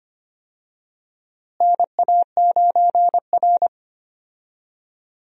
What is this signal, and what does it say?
Morse code, heard as a keyed tone, about 25 wpm, 700 Hz. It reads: NA9R
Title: USA Call Signs - 4 Characters - Repeated 2x - 25wpm